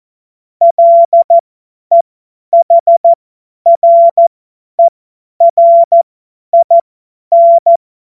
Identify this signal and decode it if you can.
Morse code, heard as a keyed tone, about 14 wpm, spaced as for 10 wpm, 680 Hz. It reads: LEHRERIN
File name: Morsekurs / Witze / Witz 13. September 2022 50 F70